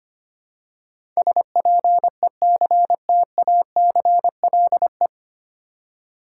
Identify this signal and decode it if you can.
Morse code, heard as a keyed tone, about 25 wpm, 700 Hz. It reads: SPECTACLE
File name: Nine Letter Words - 25wpm